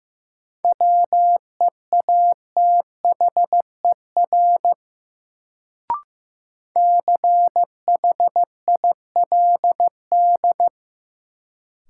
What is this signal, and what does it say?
Morse code, heard as a keyed tone, about 15 wpm, 700 Hz. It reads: WEATHER CHILD
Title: Sentences from Top 800 Words - New Word Drill - 15wpm